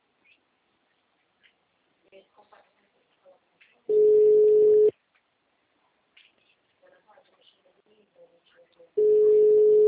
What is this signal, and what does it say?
Ringback tone (Europe, 425 Hz).
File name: out- -2002- - - .22469